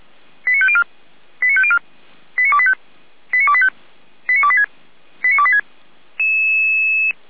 Five-tone selective call (ZVEI), heard as quick runs of five tones, tones of about 70 ms. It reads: 89504 89504 89285 89285 89286 89286
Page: audio.f85360.feuerwehr-orlamuende.org